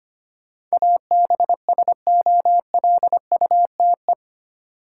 Morse code, heard as a keyed tone, about 25 wpm, 690 Hz. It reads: ABSOLUTE
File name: 8 Letter Words - First 3 Letters Given - 25wpm